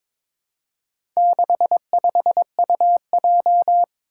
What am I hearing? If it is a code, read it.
Morse code, heard as a keyed tone, about 22 wpm, 700 Hz. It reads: 65UJ
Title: Four Challenging Letter-Number Combinations 22wpm